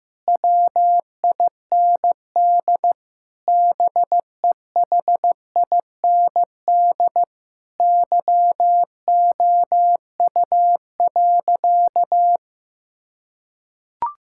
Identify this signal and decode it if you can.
Morse code, heard as a keyed tone, about 15 wpm, 700 Hz. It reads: WIND BEHIND YOU.
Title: Sentences from Top 1000 Words - Review Part 6 - 15wpm